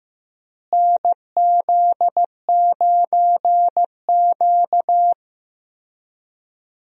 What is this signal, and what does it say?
Morse code, heard as a keyed tone, about 15 wpm, 700 Hz. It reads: NZ9Q
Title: USA Call Signs - 4 Characters - Repeated 3x - 15wpm